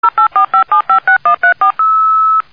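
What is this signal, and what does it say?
DTMF, heard as telephone touch tones, tones of about 110 ms. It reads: 0845756134
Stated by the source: Modem sound used as a mobile phone ringtone.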